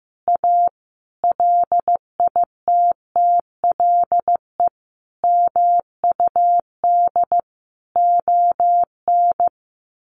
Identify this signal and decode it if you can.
Morse code, heard as a keyed tone, about 15 wpm, 700 Hz. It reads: A LITTLE MUD ON
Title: A Sound Of Thunder 15wpm